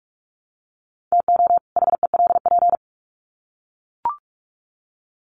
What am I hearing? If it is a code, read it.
Morse code, heard as a keyed tone, about 44 wpm, 700 Hz. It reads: TO HELP
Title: Sets of 2 Words 45wpm